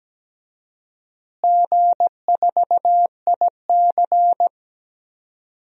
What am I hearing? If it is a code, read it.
Morse code, heard as a keyed tone, about 17 wpm, 700 Hz. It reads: G4IC
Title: DXCC-100 Call Signs - 4 Characters - Repeated 3x - 17wpm